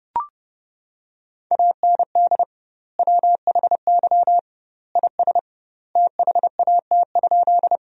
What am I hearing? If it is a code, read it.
Morse code, heard as a keyed tone, about 30 wpm, 700 Hz. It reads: AND WHY IS THAT?
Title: 2x Word-Spacing - Sentences from Top 200 Words - 30wpm